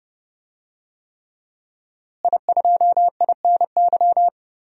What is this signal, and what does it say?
Morse code, heard as a keyed tone, about 30 wpm, 700 Hz. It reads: I2INY
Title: DXCC-100 Call Signs - 5 Characters - Repeated 3x - 30wpm